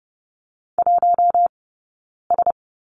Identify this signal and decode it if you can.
Morse code, heard as a keyed tone, about 30 wpm, 700 Hz. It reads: 1 S
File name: Lesson 18d - Letter m - Two Characters 30wpm